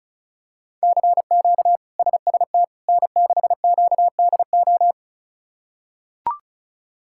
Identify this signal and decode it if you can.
Morse code, heard as a keyed tone, about 36 wpm, 700 Hz. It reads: CQ SST N6QDO
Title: K1USN SST QSO Practice - 35wpm